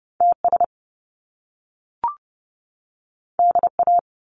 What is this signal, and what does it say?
Morse code, heard as a keyed tone, about 30 wpm, 700 Hz. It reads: TS DA
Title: Top Two Letter Combinations in Words 30wpm